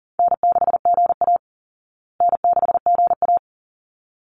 Morse code, heard as a keyed tone, about 40 wpm, 700 Hz. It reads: N6GA N6GA
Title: Easy Call Signs Repeated 2x - 40wpm